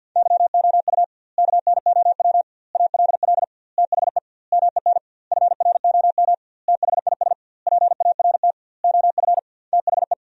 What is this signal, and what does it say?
Morse code, heard as a keyed tone, about 50 wpm, 700 Hz. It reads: YOU KNOW ALL THE MEN FROM THIS PART OF THE
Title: 2x Word-Spacing - Sentences from Top 300 Words - 50wpm